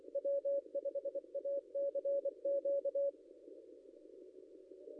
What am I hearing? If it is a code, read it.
Morse code, heard as a keyed tone, about 24 wpm, 560 Hz. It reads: W5ACQ
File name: contests / ss-cw-k5tr-op-n5rz -0700